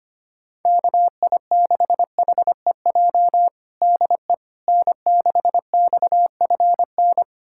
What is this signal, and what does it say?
Morse code, heard as a keyed tone, about 25 wpm, 700 Hz. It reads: KI6HEJ DE N6XFN
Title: USA Callsign DE Callsign 25wpm